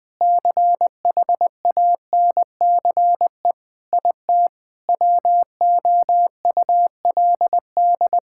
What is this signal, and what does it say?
Morse code, heard as a keyed tone, about 21 wpm, 700 Hz. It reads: CHANCE IT WOULD
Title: Sentences from Top 1000 Words - Review Part 6 - 20wpm